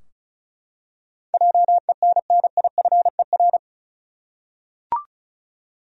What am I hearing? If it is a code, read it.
Morse code, heard as a keyed tone, about 34 wpm, 700 Hz. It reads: JENNIFER
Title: USA Common Names - 35wpm